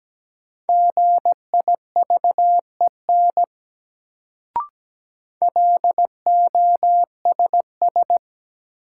Morse code, heard as a keyed tone, about 17 wpm, 700 Hz. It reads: GIVEN LOSS
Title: Sentences from Top 1400 Words - New Word Drill Alt - 17wpm